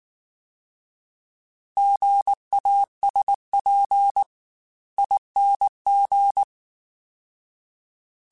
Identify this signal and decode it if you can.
Morse code, heard as a keyed tone, about 19 wpm, 800 Hz. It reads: GASP ING